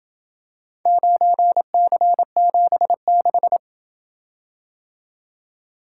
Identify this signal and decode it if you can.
Morse code, heard as a keyed tone, about 27 wpm, 700 Hz. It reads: 9C76